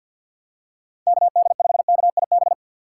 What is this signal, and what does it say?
Morse code, heard as a keyed tone, about 47 wpm, 700 Hz. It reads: KD5KIB
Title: USA Call Signs - 6 Characters - 4x Character Spacing - 50wpm